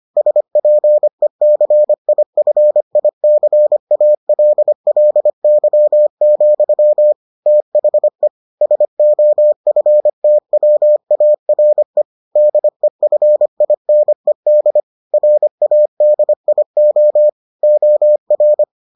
Morse code, heard as a keyed tone, about 25 wpm, 600 Hz. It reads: SPECIFICALLY, THE SOFTWARE DEFINED RADIO OR